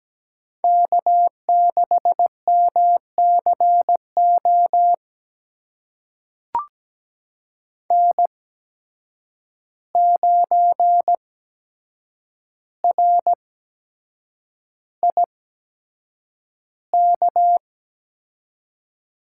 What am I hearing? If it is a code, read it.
Morse code, heard as a keyed tone, about 17 wpm, spaced as for 7 wpm, 700 Hz. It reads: K6MCO N9RIK
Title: USA Call Signs - 5 Characters - 8x Character Spacing - 17wpm